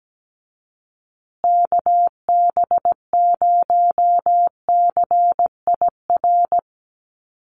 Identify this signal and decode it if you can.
Morse code, heard as a keyed tone, about 17 wpm, 700 Hz. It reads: KB0CIR
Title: USA Call Signs Repeated 3x 17wpm